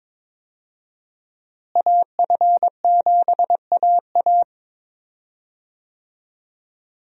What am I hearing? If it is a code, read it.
Morse code, heard as a keyed tone, about 22 wpm, 690 Hz. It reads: AF7AA
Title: USA Call Signs - 5 Characters - Repeated 3x - 22wpm